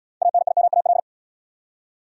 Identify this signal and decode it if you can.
Morse code, heard as a keyed tone, about 67 wpm, 700 Hz. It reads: ANIMAL